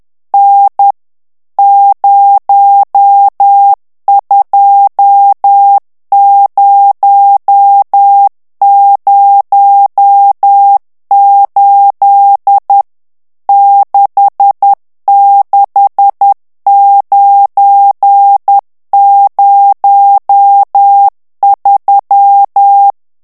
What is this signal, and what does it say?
Morse code, heard as a keyed tone, about 10 wpm, 800 Hz. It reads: N 02008 66903
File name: morse mp3 / ITALIEN texte clair 1 vitesse 10 mots minute